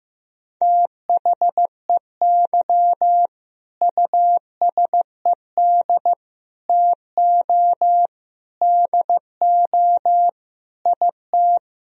Morse code, heard as a keyed tone, about 15 wpm, 700 Hz. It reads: THEY USED TO DO IT
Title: Sentences from Top 1000 Words - Review Part 4 - 15wpm